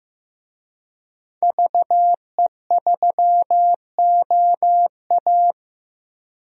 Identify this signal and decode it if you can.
Morse code, heard as a keyed tone, about 15 wpm, 700 Hz. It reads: VE3OA
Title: DXCC-100 Call Signs - 5 Characters - 15wpm